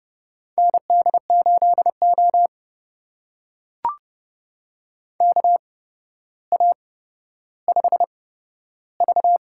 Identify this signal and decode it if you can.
Morse code, heard as a keyed tone, about 31 wpm, spaced as for 13 wpm, 700 Hz. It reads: ND8O KA5V
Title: USA Call Signs - 4 Characters - 8x Character Spacing - 30wpm